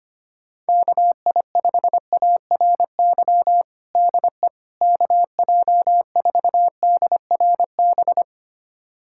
Morse code, heard as a keyed tone, about 25 wpm, 700 Hz. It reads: KI5ARY DE KJ4DRB
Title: USA Callsign DE Callsign 25wpm